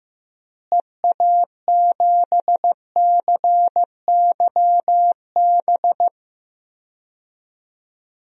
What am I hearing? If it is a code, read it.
Morse code, heard as a keyed tone, about 16 wpm, 700 Hz. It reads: EA7CYB
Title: DXCC-100 Call Signs - 6 Characters - Repeated 3x - 15wpm